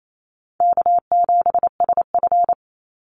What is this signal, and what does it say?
Morse code, heard as a keyed tone, about 28 wpm, 700 Hz. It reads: K7SF
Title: Easy Call Signs Repeated 3x - 28wpm